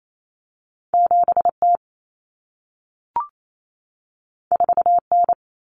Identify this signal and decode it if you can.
Morse code, heard as a keyed tone, about 28 wpm, 700 Hz. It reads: 7T 4N
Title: Common Cut Numbers - 2-Digit 28wpm